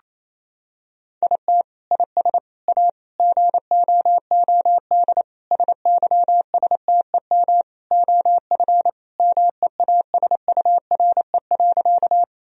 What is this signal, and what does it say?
Morse code, heard as a keyed tone, about 28 wpm, 700 Hz. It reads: IT IS A GOOD SYSTEM OF MEASURE.
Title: Sentences from Top 1000 Words - Review Part 3 - 28wpm